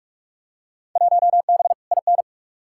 Morse code, heard as a keyed tone, about 46 wpm, 700 Hz. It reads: 1B IN
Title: Field Day Exchanges - 45wpm